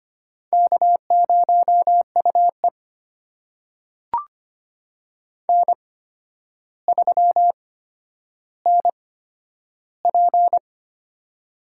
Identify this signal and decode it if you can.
Morse code, heard as a keyed tone, about 25 wpm, spaced as for 11 wpm, 700 Hz. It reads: K0UE N3NP
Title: USA Call Signs - 4 Characters - 8x Character Spacing - 25wpm